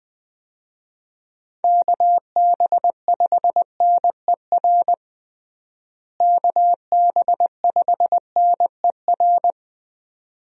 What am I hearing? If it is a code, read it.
Morse code, heard as a keyed tone, about 20 wpm, 700 Hz. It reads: KB5NER KB5NER
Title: USA Call Signs - 6 Characters - Repeated 2x - 20wpm